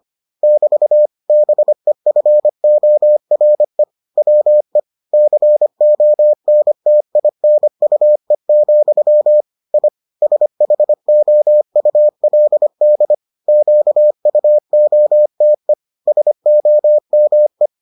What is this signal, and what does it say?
Morse code, heard as a keyed tone, about 25 wpm, 600 Hz. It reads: <BT> BEFORE WE CONTINUE, I SHOULD QUOTE SOME